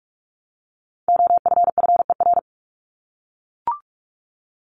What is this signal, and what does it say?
Morse code, heard as a keyed tone, about 44 wpm, 700 Hz. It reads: OFFER
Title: Top 501-750 Words 45wpm